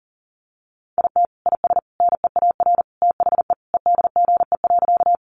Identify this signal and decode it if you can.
Morse code, heard as a keyed tone, about 40 wpm, 700 Hz. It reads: IT IS NEAR THE EDGE.
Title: Additional Easy Phrases 40wpm